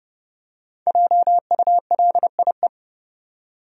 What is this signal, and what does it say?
Morse code, heard as a keyed tone, about 31 wpm, 700 Hz. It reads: JULIE
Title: USA Common Names - 30wpm